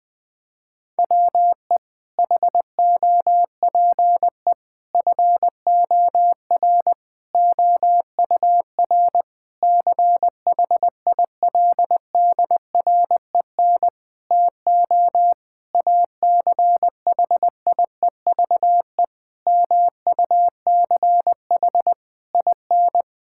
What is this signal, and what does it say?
Morse code, heard as a keyed tone, about 21 wpm, 700 Hz. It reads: WE HOPE FOR OUR CHILDREN TO ACHIEVE MUCH IN
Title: Sentences from Top 1000 Words - Review Part 6 - 20wpm